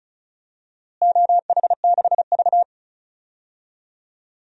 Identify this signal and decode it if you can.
Morse code, heard as a keyed tone, about 36 wpm, 700 Hz. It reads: OH6V
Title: DXCC-100 Call Signs - 4 Characters - Repeated 2x - 35wpm